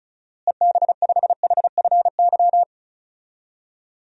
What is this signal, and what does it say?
Morse code, heard as a keyed tone, about 36 wpm, 700 Hz. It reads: EB5HFY
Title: DXCC-100 Call Signs - 6 Characters - Repeated 2x - 35wpm